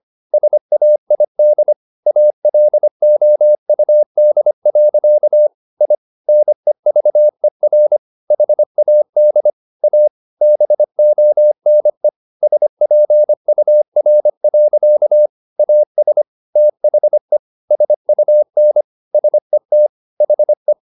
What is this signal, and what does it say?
Morse code, heard as a keyed tone, about 25 wpm, 600 Hz. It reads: SAID ALOUD. I NEVER HAD A BONE SPUR. AS THE SUN SET HE
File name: the old man and the sea part 1 25wpm